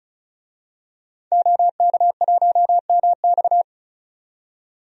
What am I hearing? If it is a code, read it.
Morse code, heard as a keyed tone, about 36 wpm, 700 Hz. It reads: OK1MX
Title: DXCC-100 Call Signs - 5 Characters - 35wpm